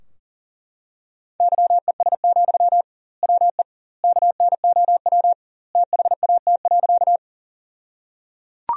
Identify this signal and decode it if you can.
Morse code, heard as a keyed tone, about 37 wpm, 700 Hz. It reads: YES, WE KNOW THAT.
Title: 2x Word-Spacing - Sentences from Top 500 Words - 40wpm